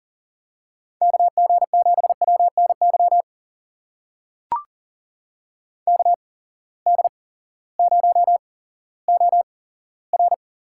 Morse code, heard as a keyed tone, about 37 wpm, spaced as for 18 wpm, 700 Hz. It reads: KG7WNY KD0OR
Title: USA Call Signs - 6 Characters - 8x Character Spacing - 40wpm